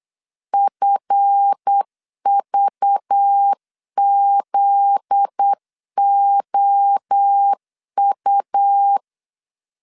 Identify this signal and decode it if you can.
Morse code, heard as a keyed tone, about 8 wpm, 800 Hz. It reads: FVZOU
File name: CW / DET / 1 9 1500C S B 8wpm 5624